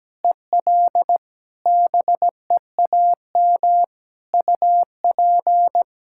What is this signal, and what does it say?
Morse code, heard as a keyed tone, about 17 wpm, 700 Hz. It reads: EL BEAM UP
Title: Practice QSO - Exchange 2 - Info - 17wpm